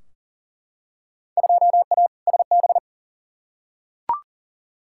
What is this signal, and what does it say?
Morse code, heard as a keyed tone, about 40 wpm, 700 Hz. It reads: 2A SB
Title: Field Day Exchanges - 40wpm